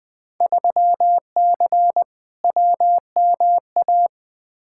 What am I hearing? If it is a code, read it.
Morse code, heard as a keyed tone, about 20 wpm, 700 Hz. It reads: 3C WMA
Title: Field Day Exchanges Encore - 20wpm